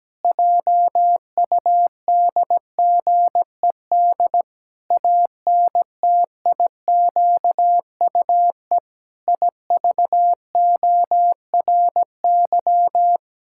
Morse code, heard as a keyed tone, about 17 wpm, 700 Hz. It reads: JUDGED ANTIQUE IVORY
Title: Pangram Challenge v2 - 17wpm